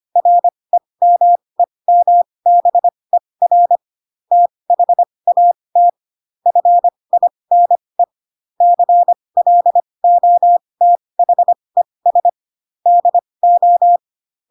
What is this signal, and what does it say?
Morse code, heard as a keyed tone, about 25 wpm, spaced as for 20 wpm, 700 Hz. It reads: REMEMBER THAT FINE CLOTHES DO
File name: Pinocchio / Pinocchio ch8 15 25 ik1hge